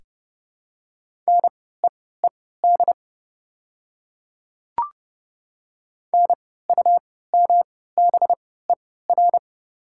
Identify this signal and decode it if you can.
Morse code, heard as a keyed tone, about 29 wpm, spaced as for 17 wpm, 690 Hz. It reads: NEED NUMBER
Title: 3x Character Spacing - Top 200 Words - 30wpm